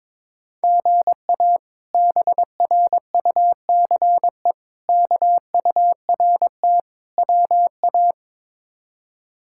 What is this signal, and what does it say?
Morse code, heard as a keyed tone, about 22 wpm, 700 Hz. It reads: GA BRUCE KURT WA
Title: K1USN SST QSO Practice - 22wpm